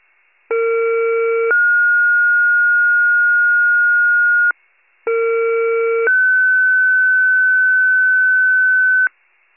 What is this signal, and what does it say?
Two-tone page, heard as a one-second tone, then a three-second tone, about 460 then 1430 Hz, about 460 then 1500 Hz. It is heard from 0.5 to 9.1 s.